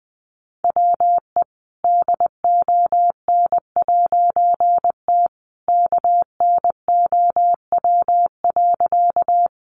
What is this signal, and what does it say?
Morse code, heard as a keyed tone, about 20 wpm, 700 Hz. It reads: WE DON'T KNOW.
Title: A Sound Of Thunder 20wpm